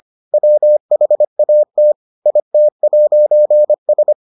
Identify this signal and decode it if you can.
Morse code, heard as a keyed tone, about 25 wpm, 600 Hz. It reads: WHAT IT'S